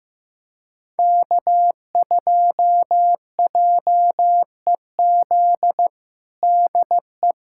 Morse code, heard as a keyed tone, about 15 wpm, 700 Hz. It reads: K2JEZ DE
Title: USA Callsign DE Callsign - 5 Characters - 15wpm